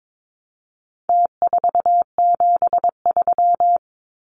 Morse code, heard as a keyed tone, about 22 wpm, 700 Hz. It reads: T473
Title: Numbers - Contest Style - 4-Digit 22wpm